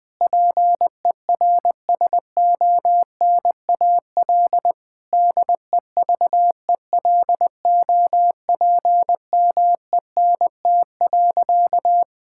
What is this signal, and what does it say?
Morse code, heard as a keyed tone, about 20 wpm, 700 Hz. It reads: PERSONAL DEVELOPMENT.
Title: Difficult Sentences with Hints - 20wpm